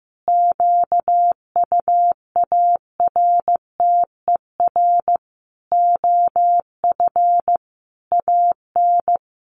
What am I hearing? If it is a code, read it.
Morse code, heard as a keyed tone, about 15 wpm, 700 Hz. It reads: QUARTER OF AN
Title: Sentences from Top 1500 Words 15wpm